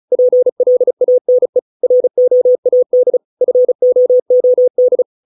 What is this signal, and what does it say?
Morse code, heard as a keyed tone, about 34 wpm, 500 Hz. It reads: PLANE ROAD FOOD